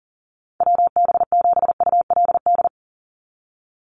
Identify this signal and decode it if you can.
Morse code, heard as a keyed tone, about 40 wpm, 700 Hz. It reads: WB7ULD
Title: NA SOTA Chaser Callsigns 40wpm